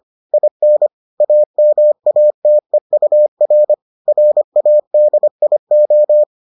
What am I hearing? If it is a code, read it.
Morse code, heard as a keyed tone, about 25 wpm, 600 Hz. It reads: IN AMATEUR RADIO